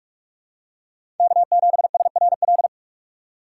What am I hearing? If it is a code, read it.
Morse code, heard as a keyed tone, about 42 wpm, 700 Hz. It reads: K7SRL
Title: USA Call Signs - 5 Characters - 2x Character Spacing - 45wpm